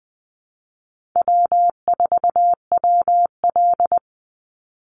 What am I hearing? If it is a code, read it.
Morse code, heard as a keyed tone, about 20 wpm, 700 Hz. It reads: W4WL
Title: USA Call Signs 20wpm